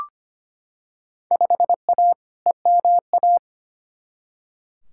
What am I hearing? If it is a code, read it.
Morse code, heard as a keyed tone, about 25 wpm, 700 Hz. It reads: 5A EMA
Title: Field Day Exchanges Encore - 25wpm